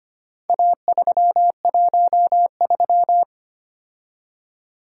Morse code, heard as a keyed tone, about 25 wpm, 700 Hz. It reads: A313
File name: practice / 25wpm / Four Letter-Number Combinations - 25wpm